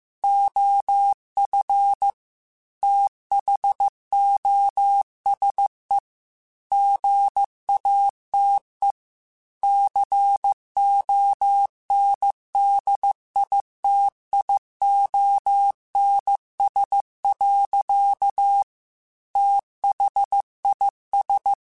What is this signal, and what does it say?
Morse code, heard as a keyed tone, about 15 wpm, 800 Hz. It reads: OF THOSE GATE CONDITIONS. THIS
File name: Technical Subjects / Technical Subjects-mp3s / AND Gate @13